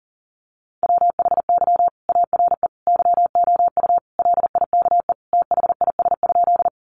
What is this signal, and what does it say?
Morse code, heard as a keyed tone, about 40 wpm, 700 Hz. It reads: WHY ARE YOU LIKE THIS?
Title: Sentences from Top 200 Words 40wpm